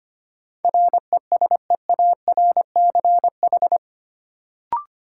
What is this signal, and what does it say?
Morse code, heard as a keyed tone, about 25 wpm, 700 Hz. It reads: RESEARCH
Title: Sentences from Top 1000 Words - Review Part 4 - New Word Drill - 25wpm